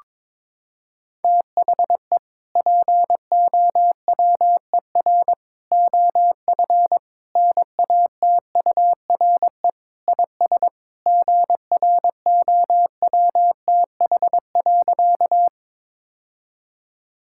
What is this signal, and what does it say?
Morse code, heard as a keyed tone, about 22 wpm, 700 Hz. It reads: THE POWER OF NATURE IS GROWTH.
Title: Sentences from Top 1000 Words - Review Part 5 - 22wpm